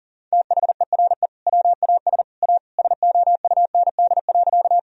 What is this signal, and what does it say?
Morse code, heard as a keyed tone, about 40 wpm, 700 Hz. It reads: THERE WAS A SOUND.
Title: Sentences from Top 1000 Words - Review Part 1 - 40wpm